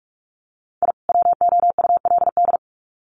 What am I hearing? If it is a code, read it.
Morse code, heard as a keyed tone, about 44 wpm, 700 Hz. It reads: I WOULD
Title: Sets of 2 Words 45wpm